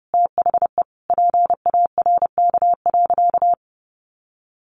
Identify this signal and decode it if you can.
Morse code, heard as a keyed tone, about 30 wpm, 700 Hz. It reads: THE PARK.
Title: Sentences from Top 1300 Words 30wpm